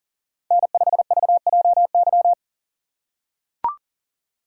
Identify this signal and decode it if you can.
Morse code, heard as a keyed tone, about 40 wpm, 700 Hz. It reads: N5VJY
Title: USA Call Signs - 6x Character Spacing - 40wpm